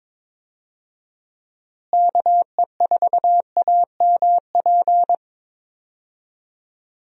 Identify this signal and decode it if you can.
Morse code, heard as a keyed tone, about 22 wpm, 700 Hz. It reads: KE4AMP
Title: USA Call Signs - 6 Characters - Repeated 2x - 22wpm